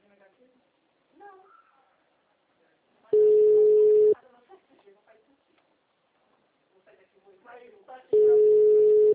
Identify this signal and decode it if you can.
Ringback tone (Europe, 425 Hz), with speech.
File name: out- -2009- - - .19462